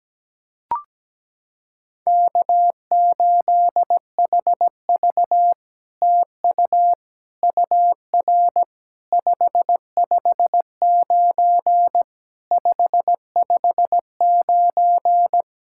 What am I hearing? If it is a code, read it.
Morse code, heard as a keyed tone, about 18 wpm, 700 Hz. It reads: K8HV TU UR 559 559
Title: SOTA QSO Practice 17wpm